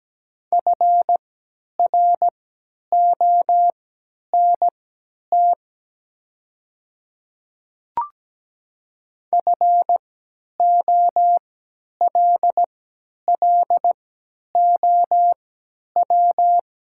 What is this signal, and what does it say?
Morse code, heard as a keyed tone, about 17 wpm, spaced as for 10 wpm, 700 Hz. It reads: FRONT FOLLOW
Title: 3x Character Spacing - Top 500 Words - 17wpm